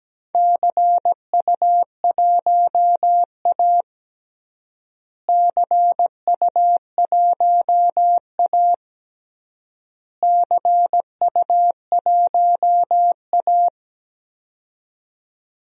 Morse code, heard as a keyed tone, about 17 wpm, 700 Hz. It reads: CU1A CU1A CU1A
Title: DXCC-100 Call Signs - 4 Characters - Repeated 3x - 17wpm